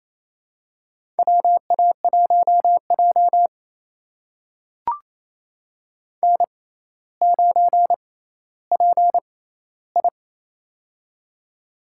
Morse code, heard as a keyed tone, about 27 wpm, spaced as for 14 wpm, 700 Hz. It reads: WA1J N9PI
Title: USA Call Signs - 4 Characters - 6x Character Spacing - 28wpm